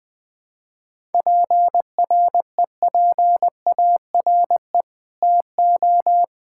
Morse code, heard as a keyed tone, about 21 wpm, 700 Hz. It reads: PREPARE TO
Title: Sentences from Top 1000 Words - Review Part 6 - 20wpm